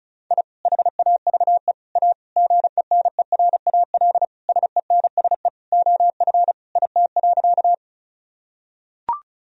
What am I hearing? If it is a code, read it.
Morse code, heard as a keyed tone, about 36 wpm, 700 Hz. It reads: I HAVE A GENERAL SENSE OF IT.
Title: Sentences from Top 1000 Words - Review Part 4 - 35wpm